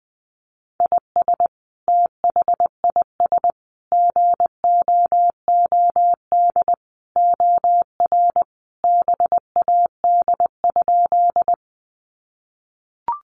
Morse code, heard as a keyed tone, about 20 wpm, 700 Hz. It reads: IS THIS GOOD OR BAD?
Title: Sentences from Top 700 Words 20wpm